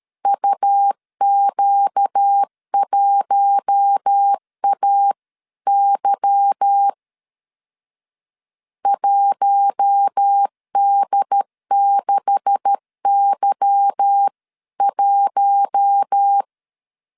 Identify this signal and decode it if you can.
Morse code, heard as a keyed tone, about 13 wpm, 800 Hz. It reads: UQ1AY 1D6Y1
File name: CW / DET / 1 9 1500C S C 12wpm 3811